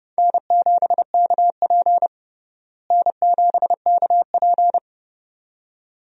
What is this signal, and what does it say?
Morse code, heard as a keyed tone, about 32 wpm, 700 Hz. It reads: N7KP N7KP
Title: USA Call Signs - 4 Characters - Repeated 2x - 30wpm